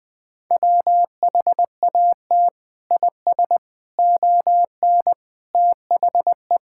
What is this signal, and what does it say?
Morse code, heard as a keyed tone, about 20 wpm, 700 Hz. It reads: WHAT IS ON THE
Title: Sentences from Top 1600 Words 20wpm